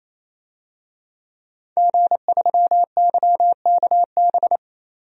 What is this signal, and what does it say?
Morse code, heard as a keyed tone, about 28 wpm, 700 Hz. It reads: G3YKB
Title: DXCC-100 Call Signs - 5 Characters - 28wpm